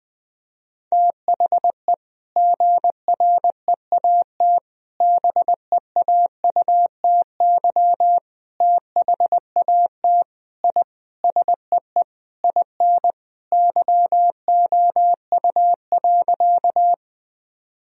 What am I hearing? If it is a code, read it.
Morse code, heard as a keyed tone, about 21 wpm, 700 Hz. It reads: THE GREAT BEAUTY THAT I SEE IN YOU.
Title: Sentences from Top 1000 Words - Review Part 2 - 20wpm